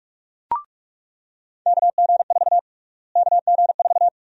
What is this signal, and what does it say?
Morse code, heard as a keyed tone, about 42 wpm, 700 Hz. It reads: KG4 KG4
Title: USA Call Sign Prefixes - Repeated 2x - 45wpm